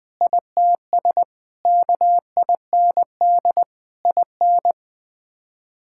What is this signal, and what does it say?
Morse code, heard as a keyed tone, about 20 wpm, 700 Hz. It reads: ITS KIND IN
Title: Sets of 6 Words - 20wpm